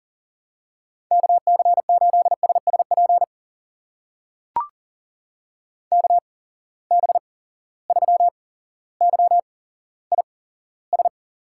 Morse code, heard as a keyed tone, about 42 wpm, spaced as for 17 wpm, 700 Hz. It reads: KC8SSP KB3YIS
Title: USA Call Signs - 8x Character Spacing - 40wpm